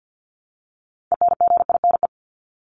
Morse code, heard as a keyed tone, about 50 wpm, 700 Hz. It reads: ENGINE
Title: Top 1251-1500 Words 50wpm